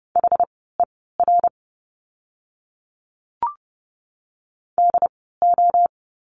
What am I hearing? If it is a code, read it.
Morse code, heard as a keyed tone, about 30 wpm, spaced as for 17 wpm, 700 Hz. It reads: HER DO
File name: course / 30:10wpm / 115 Lesson 19f - Letter w - Words 30wpm